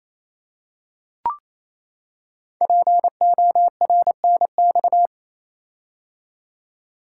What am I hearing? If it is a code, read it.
Morse code, heard as a keyed tone, about 27 wpm, 700 Hz. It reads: PORNX
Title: Code Groups - 5 Letters - 28wpm